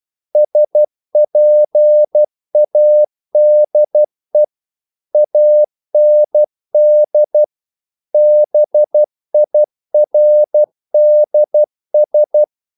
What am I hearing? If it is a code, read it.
Morse code, heard as a keyed tone, about 12 wpm, 600 Hz. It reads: SPADE AND BIRDS
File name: mp3 files / the gardener - 12wpm